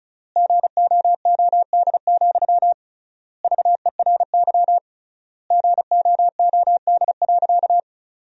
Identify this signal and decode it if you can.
Morse code, heard as a keyed tone, about 34 wpm, 700 Hz. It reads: GOOD, VERY GOOD.
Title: 3x Word-Spacing - Sentences from Top 200 Words - 35wpm